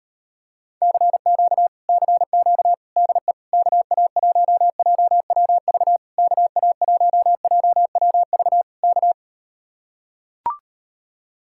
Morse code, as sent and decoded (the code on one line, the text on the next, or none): -.-. --.- / -.-. --.- / -.. / -.- .- .---- .--- .-- ...- / -.- .- .---- .--- .-- ...- / -.-
CQ CQ DE KA1JWV KA1JWV K